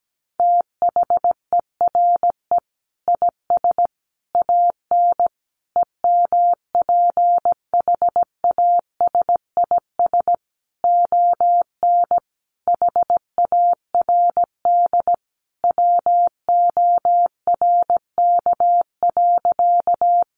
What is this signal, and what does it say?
Morse code, heard as a keyed tone, about 17 wpm, 700 Hz. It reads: THERE IS AN EMPHASIS ON HARD WORK.
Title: Sentences from Top 1400 Words 17wpm